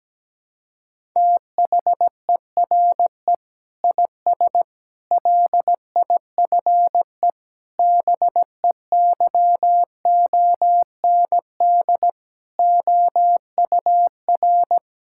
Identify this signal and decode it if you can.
Morse code, heard as a keyed tone, about 17 wpm, 700 Hz. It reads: THERE IS LIFE BEYOND OUR
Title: Sentences from Top 1000 Words - Review Part 6 - 17wpm